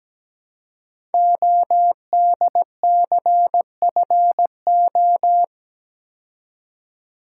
Morse code, heard as a keyed tone, about 17 wpm, 700 Hz. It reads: ODCFO
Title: Code Groups - 5 Letters - 17wpm